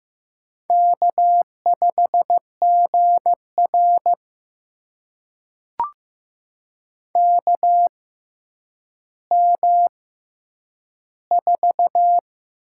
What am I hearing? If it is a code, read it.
Morse code, heard as a keyed tone, about 15 wpm, spaced as for 9 wpm, 700 Hz. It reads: K5GR KM4
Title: USA Call Signs - 6x Character Spacing - 15wpm